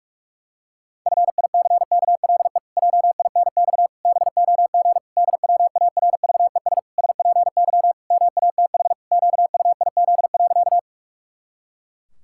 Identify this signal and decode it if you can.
Morse code, heard as a keyed tone, about 46 wpm, 700 Hz. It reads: FICKLE JINX BOG DWARVES SPY MATH QUIZ.
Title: Pangram Challenge v2 - 45wpm